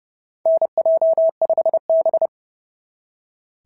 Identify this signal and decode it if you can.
Morse code, heard as a keyed tone, about 30 wpm, 660 Hz. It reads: NJ5B
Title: SOTA QSO Practice 30wpm